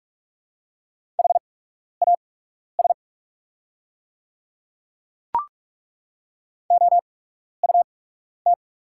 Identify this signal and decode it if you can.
Morse code, heard as a keyed tone, about 44 wpm, spaced as for 12 wpm, 690 Hz. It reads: HAS OUT
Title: 8x Character Spacing - Top 200 Words - 45wpm